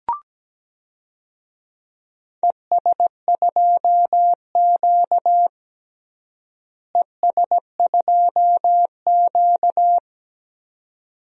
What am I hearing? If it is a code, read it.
Morse code, heard as a keyed tone, about 17 wpm, 700 Hz. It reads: ES2Q ES2Q
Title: DXCC-100 Call Signs - 4 Characters - Repeated 2x - 17wpm